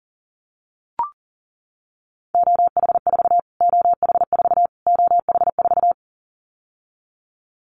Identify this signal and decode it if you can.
Morse code, heard as a keyed tone, about 40 wpm, 700 Hz. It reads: OH4 OH4 OH4
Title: Callsign Prefixes for Top 25 DXCC - Repeated 3x - 40wpm